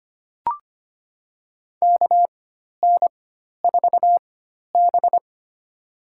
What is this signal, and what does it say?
Morse code, heard as a keyed tone, about 25 wpm, spaced as for 12 wpm, 700 Hz. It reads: KN4B
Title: USA Call Signs - 4 Characters - 4x Character Spacing - 25wpm